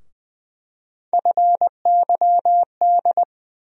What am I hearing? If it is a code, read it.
Morse code, heard as a keyed tone, about 20 wpm, 700 Hz. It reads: FYD
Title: USA Call Sign Postfixes - 3 Letters 20wpm